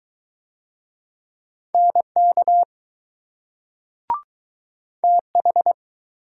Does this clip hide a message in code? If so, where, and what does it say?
Yes: Morse code from 1.7 to 5.7 s, heard as a keyed tone, about 20 wpm, 700 Hz. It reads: NK TH